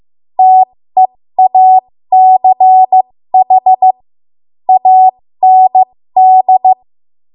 Morse code, heard as a keyed tone, about 15 wpm, spaced as for 13 wpm, 750 Hz. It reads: TEACH AND
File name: Morse / 13WPM